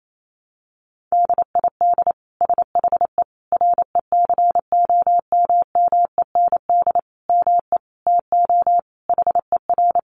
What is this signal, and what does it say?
Morse code, heard as a keyed tone, about 28 wpm, 700 Hz. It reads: DID SHE RECOMMEND ME TO HER
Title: Sentences from Top 1200 Words 28wpm